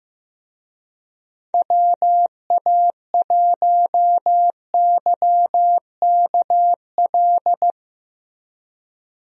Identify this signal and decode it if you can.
Morse code, heard as a keyed tone, about 15 wpm, 700 Hz. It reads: WA1YKL
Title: USA Call Signs - 6 Characters - Repeated 2x - 15wpm